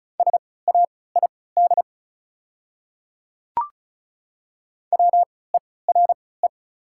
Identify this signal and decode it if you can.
Morse code, heard as a keyed tone, about 35 wpm, spaced as for 20 wpm, 690 Hz. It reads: SAID WERE
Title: 3x Character Spacing - Top 100 Words - 35wpm